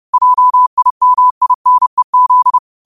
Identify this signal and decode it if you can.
Morse code, heard as a keyed tone, about 30 wpm, 1000 Hz. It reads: JIMINEZ